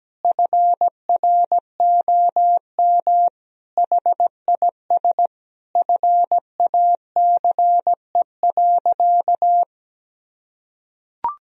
Morse code, heard as a keyed tone, about 17 wpm, 700 Hz. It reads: FROM HIS FACE.
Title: Sentences from Top 1000 Words - Review Part 2 - 17wpm